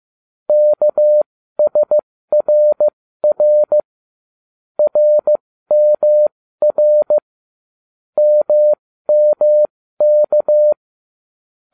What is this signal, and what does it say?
Morse code, heard as a keyed tone, about 15 wpm, spaced as for 13 wpm, 600 Hz. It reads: KSRR RMR MMK